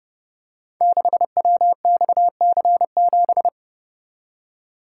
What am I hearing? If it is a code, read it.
Morse code, heard as a keyed tone, about 30 wpm, 700 Hz. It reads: 6WXC7
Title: Code Groups - 5 Letter:Numbers - 30wpm